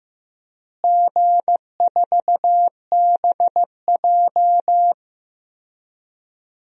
Morse code, heard as a keyed tone, about 15 wpm, 700 Hz. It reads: G4BJ
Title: DXCC-100 Call Signs - 4 Characters - Repeated 2x - 15wpm